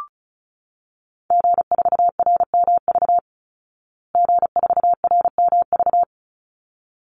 Morse code, heard as a keyed tone, about 34 wpm, 700 Hz. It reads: G4RMV G4RMV
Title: Call Signs Repeated 3x - 35wpm